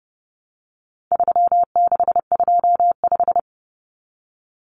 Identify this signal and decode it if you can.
Morse code, heard as a keyed tone, about 30 wpm, 700 Hz. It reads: 3625
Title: Common Cut Numbers - 4-Digit 30wpm